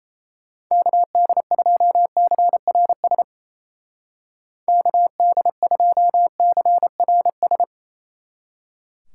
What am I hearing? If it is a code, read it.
Morse code, heard as a keyed tone, about 36 wpm, 700 Hz. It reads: KD2CRS KD2CRS